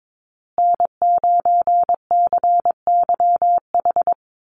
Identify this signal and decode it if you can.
Morse code, heard as a keyed tone, about 22 wpm, 700 Hz. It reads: N9CYH
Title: USA Call Signs Repeated 3x 22wpm